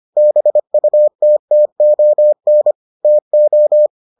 Morse code, heard as a keyed tone, about 25 wpm, 600 Hz. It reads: BUTTON TO